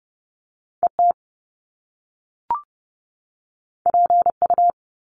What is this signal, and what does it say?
Morse code, heard as a keyed tone, about 30 wpm, 700 Hz. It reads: ET PU
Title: Top Two Letter Combinations in Words 30wpm